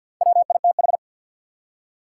Morse code, heard as a keyed tone, about 50 wpm, 700 Hz. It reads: WITH
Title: Top 180 QSO Elements 50wpm